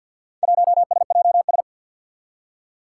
Morse code, heard as a keyed tone, about 50 wpm, 700 Hz. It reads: 1SJS
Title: Four Challenging Letter-Number Combinations 50wpm